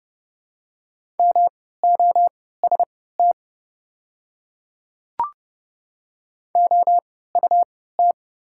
Morse code, heard as a keyed tone, about 32 wpm, spaced as for 17 wpm, 700 Hz. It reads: MOST OUT
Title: 3x Character Spacing - Top 200 Words - 30wpm